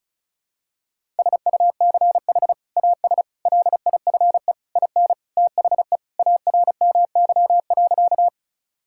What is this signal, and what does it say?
Morse code, heard as a keyed tone, about 36 wpm, 700 Hz. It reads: SUCH AS LIFE IN THE ARMY.
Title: Sentences from Top 1000 Words - Review Part 5 - 35wpm